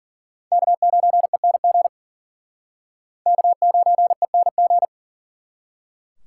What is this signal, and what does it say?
Morse code, heard as a keyed tone, about 50 wpm, 700 Hz. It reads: K9ENG K9ENG
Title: Speed Racing - USA 1x3 Call Signs 40wpm